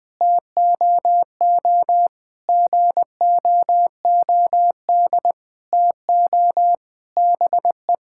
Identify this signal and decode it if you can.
Morse code, heard as a keyed tone, about 20 wpm, 700 Hz. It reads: TOO GOOD TO BE
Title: Sentences with 9-Word Phrases - 20wpm